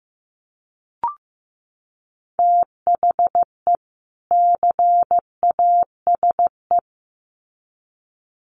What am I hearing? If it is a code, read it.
Morse code, heard as a keyed tone, about 15 wpm, 700 Hz. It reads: THE CASE
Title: Sets of 2 Words 15wpm